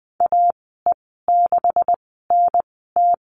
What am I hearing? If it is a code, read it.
Morse code, heard as a keyed tone, about 20 wpm, spaced as for 14 wpm, 700 Hz. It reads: AE6NT
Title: 140 Lesson 23g - Letter f - Call Signs 20wpm